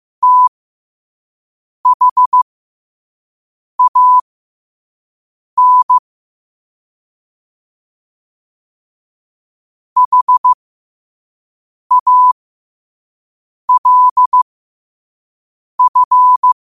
Morse code, heard as a keyed tone, about 15 wpm, spaced as for 5 wpm, 1000 Hz. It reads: THAN HALF
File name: qotdfiles / 5wpm